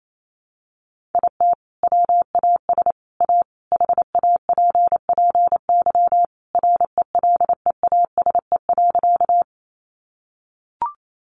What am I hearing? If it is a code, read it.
Morse code, heard as a keyed tone, about 28 wpm, 700 Hz. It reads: IT WAS A HAPPY RELEASE.
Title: Sentences from Top 1100 Words 28wpm